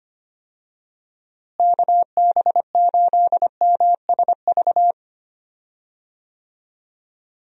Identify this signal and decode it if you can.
Morse code, heard as a keyed tone, about 25 wpm, 700 Hz. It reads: KB8MSV
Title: USA Call Signs - 6 Characters - Repeated 2x - 25wpm